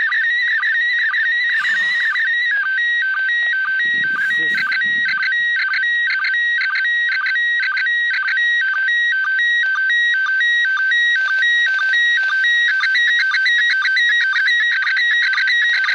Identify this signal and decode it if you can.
SSTV (slow-scan television), PD 120, with speech.
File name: 业余无线电 / ●未分类 / ISS2019-8-2-13-44过境SSTV12-12录音